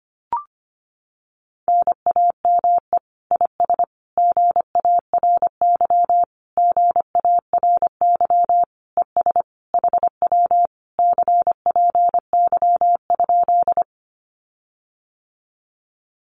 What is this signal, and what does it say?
Morse code, heard as a keyed tone, about 25 wpm, 700 Hz. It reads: NAME IS GARY GARY ES HW CPY?
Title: Common QSO Phrases Encore 25wpm